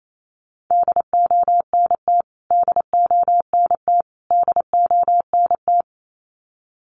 Morse code, heard as a keyed tone, about 28 wpm, 700 Hz. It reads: DONT DONT DONT
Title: Top 300 Words Repeated 28wpm